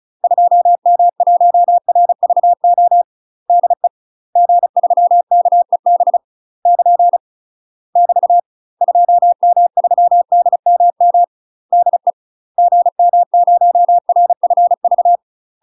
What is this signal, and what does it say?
Morse code, heard as a keyed tone, about 36 wpm, 700 Hz. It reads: 2M1RVO DE G3KEB <KN> <BT> 2M3DMM DE GM0RFV
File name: UK MASTER QSOs TRAINING / 35WPM / UK MASTER QSOs 0032